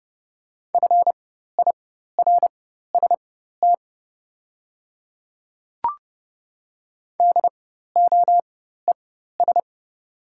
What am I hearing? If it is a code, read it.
Morse code, heard as a keyed tone, about 30 wpm, spaced as for 14 wpm, 700 Hz. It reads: FIRST DOES
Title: 4x Character Spacing - Top 300 Words - 30wpm